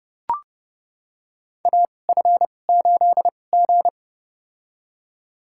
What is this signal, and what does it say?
Morse code, heard as a keyed tone, about 30 wpm, spaced as for 22 wpm, 700 Hz. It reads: AF8G
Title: USA Call Signs - 4 Characters - 2x Character Spacing - 30wpm